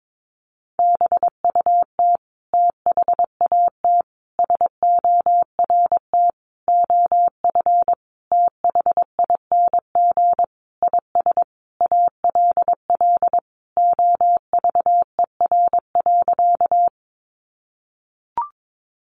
Morse code, heard as a keyed tone, about 22 wpm, 700 Hz. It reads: BUT THAT SORT OF THING IS ALL OVER.
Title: Sentences from Top 1300 Words 22wpm